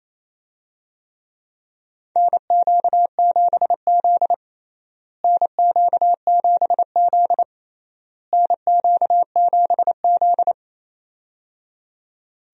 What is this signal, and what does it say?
Morse code, heard as a keyed tone, about 28 wpm, 700 Hz. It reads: NQ7Z NQ7Z NQ7Z
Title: USA Call Signs - 4 Characters - Repeated 3x - 28wpm